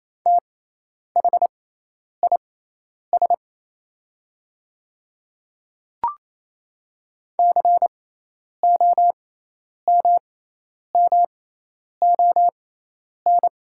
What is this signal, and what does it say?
Morse code, heard as a keyed tone, about 28 wpm, spaced as for 10 wpm, 700 Hz. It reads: THIS COMMON
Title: 6x Character Spacing - Top 500 Words - 28wpm